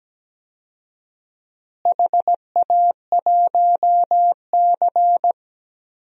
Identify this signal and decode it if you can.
Morse code, heard as a keyed tone, about 17 wpm, 700 Hz. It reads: HA1C
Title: DXCC-100 Call Signs - 4 Characters - Repeated 3x - 17wpm